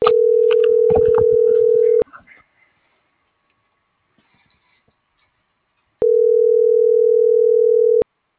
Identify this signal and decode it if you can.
Ringback tone (North America), 440 + 480 Hz.